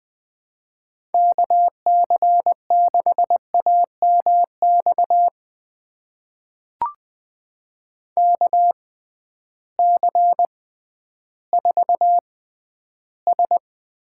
Morse code, heard as a keyed tone, about 21 wpm, spaced as for 12 wpm, 700 Hz. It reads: KC6AMX KC4S
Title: USA Call Signs - 6 Characters - 6x Character Spacing - 20wpm